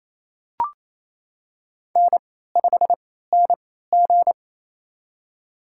Morse code, heard as a keyed tone, about 27 wpm, spaced as for 16 wpm, 700 Hz. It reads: N5NG
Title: USA Call Signs - 4 Characters - 3x Character Spacing - 28wpm